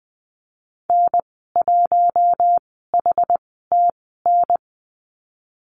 Morse code, heard as a keyed tone, about 20 wpm, spaced as for 14 wpm, 700 Hz. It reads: N1HTN
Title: 088 Lesson 15g - Number 5 - Call Signs 20wpm